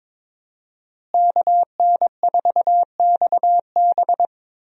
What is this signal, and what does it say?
Morse code, heard as a keyed tone, about 22 wpm, 700 Hz. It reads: KN4XB
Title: USA Call Signs - 5 Characters - Repeated 3x - 22wpm